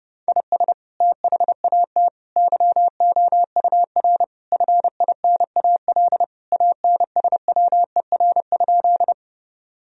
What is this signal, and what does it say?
Morse code, heard as a keyed tone, about 30 wpm, 700 Hz. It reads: IS THAT YOUR FINAL ANSWER?
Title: Sentences from Top 1000 Words - Review Part 2 - 30wpm